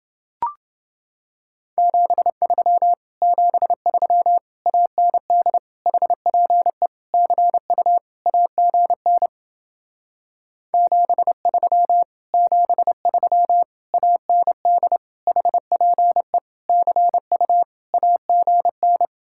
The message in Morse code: 73 73 AND HPE CU AGN 73 73 AND HPE CU AGN

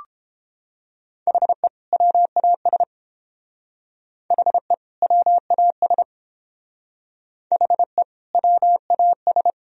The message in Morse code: HE WAS HE WAS HE WAS

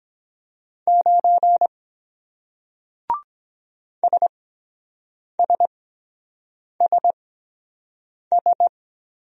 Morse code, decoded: 9 SSSS

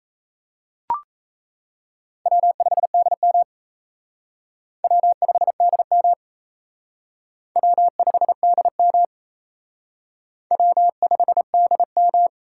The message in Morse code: W5DM W5DM W5DM W5DM

